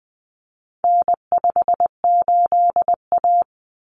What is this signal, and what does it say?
Morse code, heard as a keyed tone, about 20 wpm, 700 Hz. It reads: N58A
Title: Common Cut Numbers - 4-Digit 20wpm